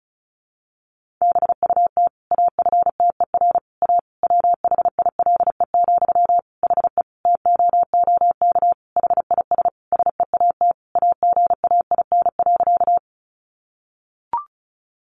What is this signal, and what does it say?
Morse code, heard as a keyed tone, about 34 wpm, 700 Hz. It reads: BUT AFTER A WHILE, HE TOOK HIS SEAT AGAIN.
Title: Sentences from Top 1500 Words 35wpm